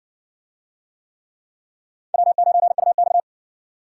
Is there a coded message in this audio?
Yes: Morse code from 2.1 to 3.2 s, heard as a keyed tone, about 60 wpm, 700 Hz. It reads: W9UX